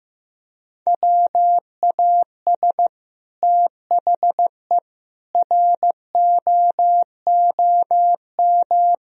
Morse code, heard as a keyed tone, about 15 wpm, 700 Hz. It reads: WAS THE ROOM
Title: Sentences from Top 1000 Words - Review Part 4 - 15wpm